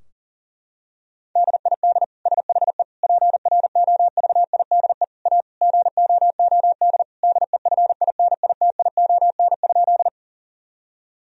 Morse code, heard as a keyed tone, about 37 wpm, 700 Hz. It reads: DID SHE PROVIDE A GOOD DEFINITION?
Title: Sentences from Top 1000 Words - Review Part 4 - 40wpm